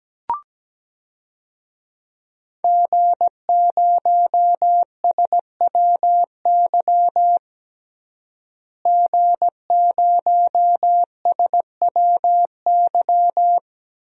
Morse code, heard as a keyed tone, about 17 wpm, 700 Hz. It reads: G0SWY G0SWY
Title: DXCC-100 Call Signs - 5 Characters - Repeated 2x - 17wpm